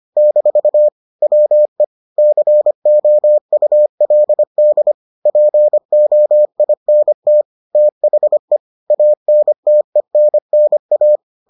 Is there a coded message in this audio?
Yes: Morse code from 0.2 to 11.2 s, heard as a keyed tone, about 25 wpm, 600 Hz. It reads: - WE COULD POINT THE ANTENNA